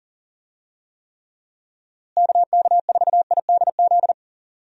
Morse code, heard as a keyed tone, about 37 wpm, 700 Hz. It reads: KK4IDZ
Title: USA Call Signs - 6 Characters - Repeated 2x - 40wpm